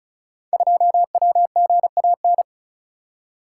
Morse code, heard as a keyed tone, about 36 wpm, 700 Hz. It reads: 2WGAN
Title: Code Groups - 5 Letter:Numbers - 35wpm